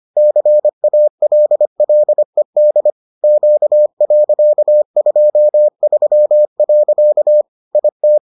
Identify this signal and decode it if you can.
Morse code, heard as a keyed tone, about 25 wpm, 600 Hz. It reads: CALLED Q.23. IT